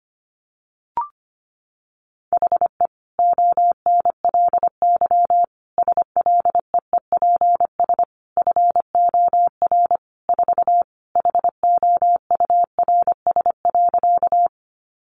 Morse code, heard as a keyed tone, about 25 wpm, 700 Hz. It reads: HE ONLY SLEEPS FOR 4 HOURS.
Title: Additional Easy Phrases 25wpm